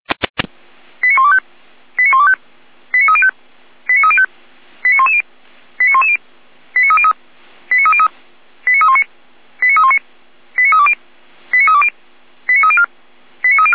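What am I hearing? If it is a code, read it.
Five-tone selective call (ZVEI), heard as quick runs of five tones, tones of about 80 ms. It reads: 89125 89125 89305 89305 89110 89110 89303 89303 89319 89319 89320 89320 89304 89304
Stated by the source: ZVEI five-tone call received on SDR.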